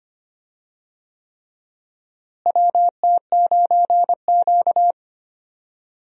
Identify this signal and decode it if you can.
Morse code, heard as a keyed tone, about 25 wpm, 700 Hz. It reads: WT9Q
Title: USA Call Signs - 4 Characters - Repeated 3x - 25wpm